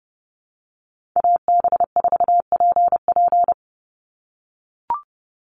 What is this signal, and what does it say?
Morse code, heard as a keyed tone, about 30 wpm, 700 Hz. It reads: AB4PP
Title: NA SOTA Chaser Callsigns 30wpm